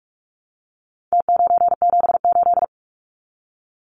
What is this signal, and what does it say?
Morse code, heard as a keyed tone, about 44 wpm, 700 Hz. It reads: T978
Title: Numbers - Contest Style - 4-Digit 45wpm